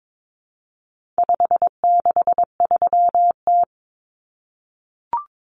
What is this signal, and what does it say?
Morse code, heard as a keyed tone, about 22 wpm, 700 Hz. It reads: 563T
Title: Common Cut Numbers - 4-Digit 22wpm